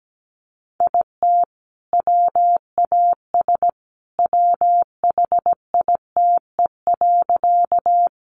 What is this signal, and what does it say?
Morse code, heard as a keyed tone, about 17 wpm, 700 Hz. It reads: IT WAS WHITE.
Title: Sentences from Top 300 Words 17wpm